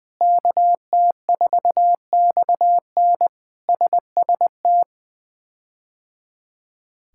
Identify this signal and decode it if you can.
Morse code, heard as a keyed tone, about 20 wpm, 700 Hz. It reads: KT4XN SST
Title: K1USN SST QSO Practice - 20wpm